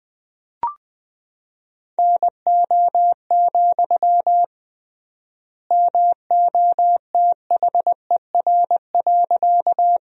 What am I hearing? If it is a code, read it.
Morse code, heard as a keyed tone, about 19 wpm, 700 Hz. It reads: NO, MOTHER.
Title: 3x Word-Spacing - Sentences from Top 200 Words - 20wpm